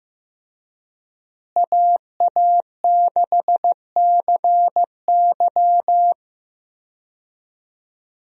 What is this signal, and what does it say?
Morse code, heard as a keyed tone, about 15 wpm, 700 Hz. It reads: AA6CY
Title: USA Call Signs - 5 Characters - Repeated 3x - 15wpm